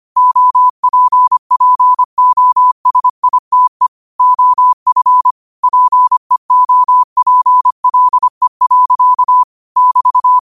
Morse code, heard as a keyed tone, about 25 wpm, 1000 Hz. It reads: OPPOSITE OF PEOPLE. <BT>